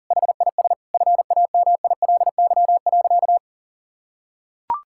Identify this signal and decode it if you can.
Morse code, heard as a keyed tone, about 40 wpm, 700 Hz. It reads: HIS FAMILY.
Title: Sentences from Top 1000 Words - Review Part 5 - 40wpm